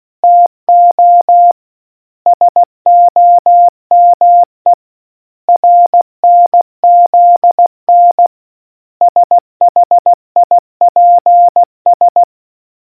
Morse code, heard as a keyed tone, about 16 wpm, 700 Hz. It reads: TO SOME RNZN SHIPS